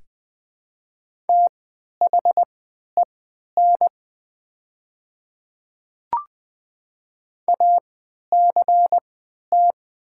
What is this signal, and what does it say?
Morse code, heard as a keyed tone, about 20 wpm, spaced as for 11 wpm, 700 Hz. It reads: THEN ACT
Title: 3x Character Spacing - Top 300 Words - 20wpm